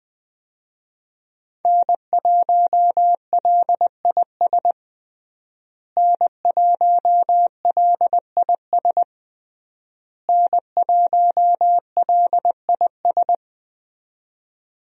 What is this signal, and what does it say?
Morse code, heard as a keyed tone, about 20 wpm, 700 Hz. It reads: N1LIS N1LIS N1LIS
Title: USA Call Signs - 5 Characters - Repeated 3x - 20wpm